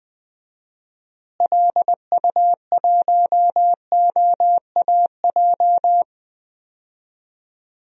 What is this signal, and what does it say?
Morse code, heard as a keyed tone, about 20 wpm, 700 Hz. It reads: LU1OAJ
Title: DXCC-100 Call Signs - 6 Characters - Repeated 2x - 20wpm